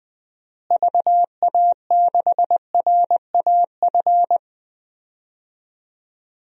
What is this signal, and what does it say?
Morse code, heard as a keyed tone, about 20 wpm, 700 Hz. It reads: VA6RAF
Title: DXCC-100 Call Signs - 6 Characters - Repeated 2x - 20wpm